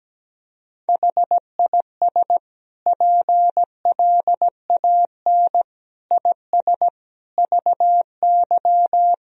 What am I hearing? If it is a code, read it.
Morse code, heard as a keyed tone, about 17 wpm, 700 Hz. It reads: HIS PLAN IS VY
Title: Sentences from Top 500 Words and Abbreviations 17wpm